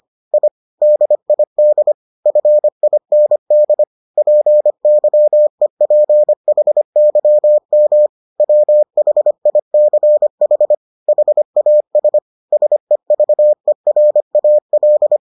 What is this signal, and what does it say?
Morse code, heard as a keyed tone, about 25 wpm, 600 Hz. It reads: I DID FIND PYEPHYM WHICH HAS SEVERAL